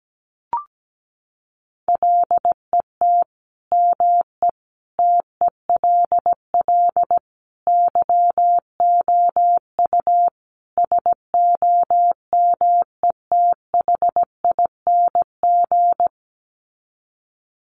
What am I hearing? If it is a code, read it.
Morse code, heard as a keyed tone, about 17 wpm, 700 Hz. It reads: LET ME TELL YOU SOMETHING
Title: Sets of 5 Words 17wpm